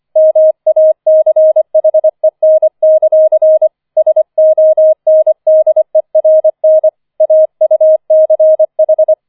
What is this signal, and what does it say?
Morse code, heard as a keyed tone, about 25 wpm, 630 Hz. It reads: MACHEN; SONDERN AUCH